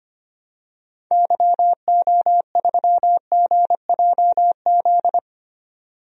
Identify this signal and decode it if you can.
Morse code, heard as a keyed tone, about 25 wpm, 700 Hz. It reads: YO3GJZ
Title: DXCC-100 Call Signs - 6 Characters - 25wpm